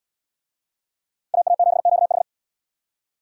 Morse code, heard as a keyed tone, about 67 wpm, 700 Hz. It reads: AA7QH